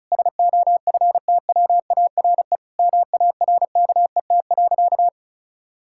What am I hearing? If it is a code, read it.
Morse code, heard as a keyed tone, about 34 wpm, 700 Hz. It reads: SOFTWARE MARKET.
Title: Sentences from Top 1000 Words - Review Part 6 - 35wpm